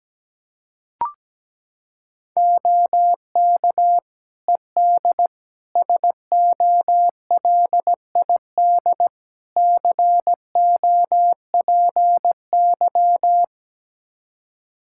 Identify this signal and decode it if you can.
Morse code, heard as a keyed tone, about 17 wpm, 700 Hz. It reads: OK ED SOLID COPY
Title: K1USN SST QSO Practice 17wpm